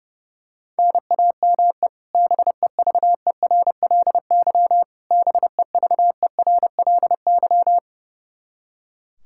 Morse code, heard as a keyed tone, about 32 wpm, 700 Hz. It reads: NAME BEVERLY BEVERLY
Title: Practice QSO - Exchange 1 - Beginning - 30wpm